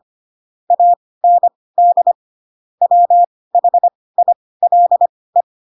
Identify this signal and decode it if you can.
Morse code, heard as a keyed tone, about 25 wpm, spaced as for 18 wpm, 700 Hz. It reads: AND WHILE
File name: Pinocchio / Pinocchio ch6 12 25 ik1hge